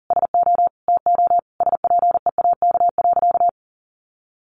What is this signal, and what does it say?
Morse code, heard as a keyed tone, about 40 wpm, 700 Hz. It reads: SO TO SPEAK.
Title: Sentences from Top 1400 Words 40wpm